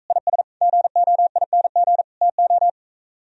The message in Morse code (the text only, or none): IS GOING TO